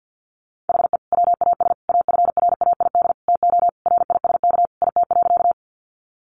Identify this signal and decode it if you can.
Morse code, heard as a keyed tone, about 50 wpm, 700 Hz. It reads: HE WAS AFRAID TO RISK IT.
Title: Sentences from Top 1500 Words 50wpm